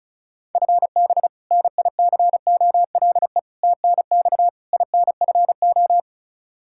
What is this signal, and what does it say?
Morse code, heard as a keyed tone, about 34 wpm, 700 Hz. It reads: FB NICOLE TNX INFO
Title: Practice QSO - Exchange 3 - More Info - 35wpm